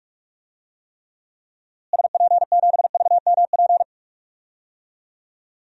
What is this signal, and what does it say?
Morse code, heard as a keyed tone, about 46 wpm, 700 Hz. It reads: SP7VMP
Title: DXCC-100 Call Signs - 6 Characters - Repeated 3x - 45wpm